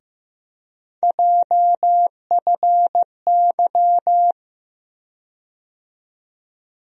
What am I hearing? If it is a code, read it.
Morse code, heard as a keyed tone, about 16 wpm, 700 Hz. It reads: JFY
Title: USA Call Sign Postfixes - 3 Letters 15wpm